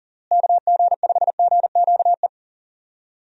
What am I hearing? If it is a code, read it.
Morse code, heard as a keyed tone, about 40 wpm, 700 Hz. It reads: KG5GQE
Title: USA Call Signs - 6 Characters - Repeated 3x - 40wpm